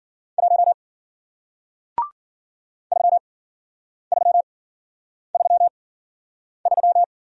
Morse code, heard as a keyed tone, about 54 wpm, spaced as for 9 wpm, 700 Hz. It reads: 1 3333